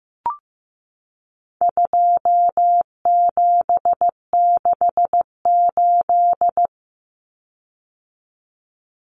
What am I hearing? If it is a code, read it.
Morse code, heard as a keyed tone, about 15 wpm, 700 Hz. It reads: 2768
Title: Common Cut Numbers - 4-Digit 15wpm